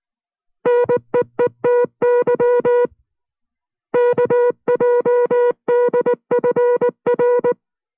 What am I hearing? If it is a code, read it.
Morse code, heard as a keyed tone, about 19 wpm, 490 Hz. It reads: NEETY KJDFR